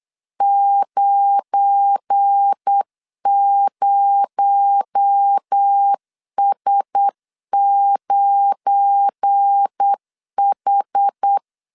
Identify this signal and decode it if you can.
Morse code, heard as a keyed tone, about 8 wpm, 800 Hz. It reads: 90S9H
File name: LEZIONI / 7 GF9Z / 7 GF9 8wpm 3545